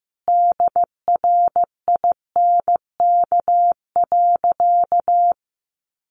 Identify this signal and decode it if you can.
Morse code, heard as a keyed tone, about 15 wpm, 700 Hz. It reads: DRINK.
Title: Sentences from Top 1500 Words 15wpm